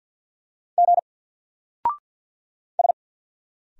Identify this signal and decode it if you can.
Morse code, heard as a keyed tone, about 49 wpm, 700 Hz. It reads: G S
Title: Single Character (Letter:Number:Punctuation:Prosign) 50wpm